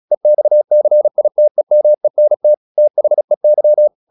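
Morse code, heard as a keyed tone, about 36 wpm, 600 Hz. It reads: EXCITEMENT THEY
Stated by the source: Morse practice at 36 wpm, text of Ken Follett's novel Never, from CW med Gnister.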